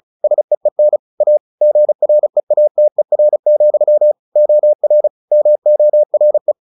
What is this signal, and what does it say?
Morse code, heard as a keyed tone, about 34 wpm, 600 Hz. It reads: SEEN A GREATER, OR MORE
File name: the old man and the sea 2 35wpm